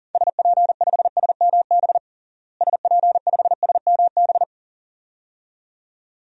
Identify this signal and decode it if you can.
Morse code, heard as a keyed tone, about 40 wpm, 700 Hz. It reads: SP5SMB SP5SMB
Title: DXCC-100 Call Signs - 6 Characters - Repeated 2x - 40wpm